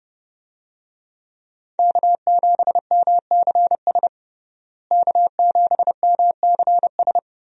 Morse code, heard as a keyed tone, about 30 wpm, 700 Hz. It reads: K7MCS K7MCS
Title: USA Call Signs - 5 Characters - Repeated 3x - 30wpm